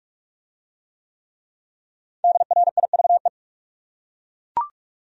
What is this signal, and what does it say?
Morse code, heard as a keyed tone, about 42 wpm, 700 Hz. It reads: DRIVE